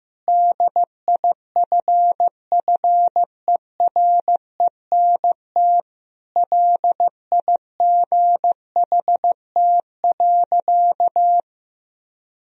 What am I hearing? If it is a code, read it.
Morse code, heard as a keyed tone, about 15 wpm, 700 Hz. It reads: DIFFERENT LIGHT.
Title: Sentences from Top 1000 Words - Review Part 3 - 15wpm